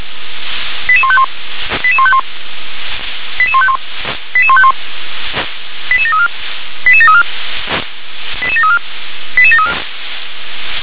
Five-tone selective call (ZVEI), heard as quick runs of five tones, tones of about 70 ms. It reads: 88151 88151 88151 88151 88734 88734 88734 88734
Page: /357/SDR/ZVEI/88748/2021.08.30/